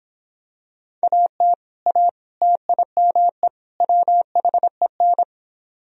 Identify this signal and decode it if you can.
Morse code, heard as a keyed tone, about 26 wpm, 700 Hz. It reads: AT A TIME WHEN